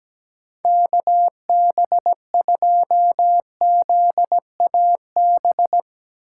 Morse code, heard as a keyed tone, about 17 wpm, 700 Hz. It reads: KB2ZAB
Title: USA Call Signs - 6 Characters - 8x Character Spacing - 17wpm